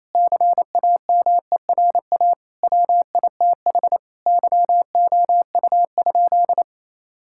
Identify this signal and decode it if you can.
Morse code, heard as a keyed tone, about 28 wpm, 700 Hz. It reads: CAMERA WITH YOU?
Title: Sentences from Top 1000 Words - Review Part 5 - 28wpm